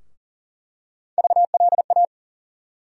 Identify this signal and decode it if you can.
Morse code, heard as a keyed tone, about 40 wpm, 700 Hz. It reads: VLA
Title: USA Call Sign Postfixes - 3 Letters 40wpm